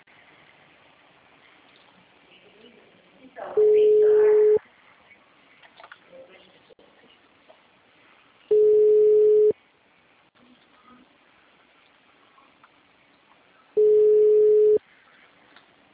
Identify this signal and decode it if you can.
Ringback tone (Europe, 425 Hz), with speech.